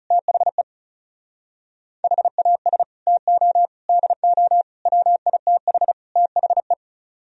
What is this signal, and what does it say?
Morse code, heard as a keyed tone, about 38 wpm, 700 Hz. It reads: THE HAS TO DO WITH THE